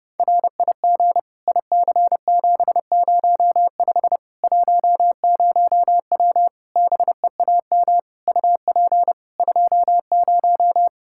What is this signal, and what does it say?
Morse code, heard as a keyed tone, about 30 wpm, 700 Hz. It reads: RIG IC705 10W BEAM UP 20
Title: Practice QSO - Exchange 2 - Info - 30wpm